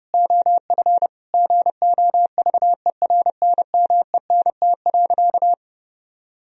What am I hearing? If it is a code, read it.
Morse code, heard as a keyed tone, about 30 wpm, 700 Hz. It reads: OF GOVERNMENT.
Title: Sentences from Top 1000 Words - Review Part 3 - 30wpm